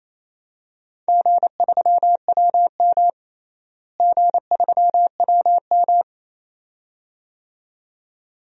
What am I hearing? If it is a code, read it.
Morse code, heard as a keyed tone, about 28 wpm, 700 Hz. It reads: G3WM G3WM
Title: DXCC-100 Call Signs - 4 Characters - Repeated 2x - 28wpm